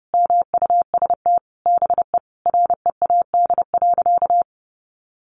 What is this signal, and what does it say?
Morse code, heard as a keyed tone, about 30 wpm, 700 Hz. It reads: MUST BE READ.
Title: Sentences from Top 1100 Words 30wpm